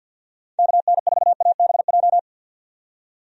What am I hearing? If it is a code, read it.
Morse code, heard as a keyed tone, about 53 wpm, 700 Hz. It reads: KN4ABJ
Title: USA Call Signs - 6 Characters - 50wpm